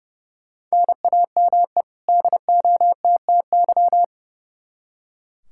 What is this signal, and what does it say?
Morse code, heard as a keyed tone, about 30 wpm, 700 Hz. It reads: NAME DOTTY
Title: K1USN SST QSO Practice 30wpm